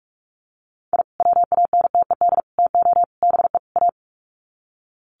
Morse code, heard as a keyed tone, about 44 wpm, 700 Hz. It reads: I WANTED TO BE A
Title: Sets of 5 Words Encore2 45wpm